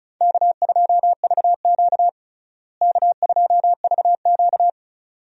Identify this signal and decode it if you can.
Morse code, heard as a keyed tone, about 34 wpm, 700 Hz. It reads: K2VQ K2VQ
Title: USA Call Signs - 4 Characters - Repeated 2x - 35wpm